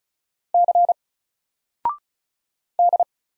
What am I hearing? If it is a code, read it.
Morse code, heard as a keyed tone, about 35 wpm, 700 Hz. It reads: C D
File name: Single-Letter-Number-35wpm